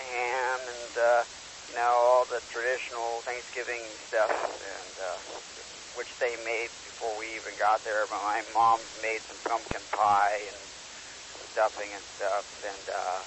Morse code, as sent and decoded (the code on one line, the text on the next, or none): none